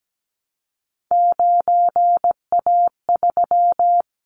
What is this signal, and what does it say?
Morse code, heard as a keyed tone, about 17 wpm, 700 Hz. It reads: 9A3
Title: Callsign Prefixes for Top 50 DXCC - 17wpm